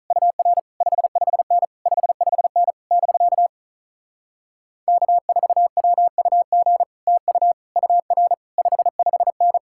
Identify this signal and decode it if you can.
Morse code, heard as a keyed tone, about 46 wpm, 700 Hz. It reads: UR 55N 55N <BK> K4WUG TU UR 55N